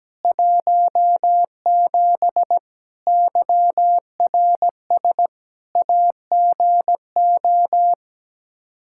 Morse code, heard as a keyed tone, about 17 wpm, 700 Hz. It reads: 17 YRS AGO
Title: K1USN SST QSO Practice 17wpm